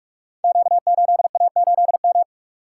Morse code, heard as a keyed tone, about 46 wpm, 700 Hz. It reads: Q8A8M
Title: Code Groups - 5 Letter:Numbers - 45wpm